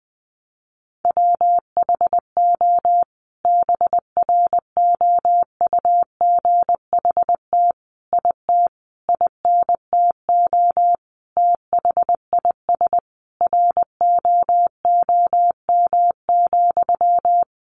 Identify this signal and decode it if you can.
Morse code, heard as a keyed tone, about 20 wpm, 700 Hz. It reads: WHO BROUGHT IT INTO THIS ROOM,
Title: Sentences from Top 600 Words 20wpm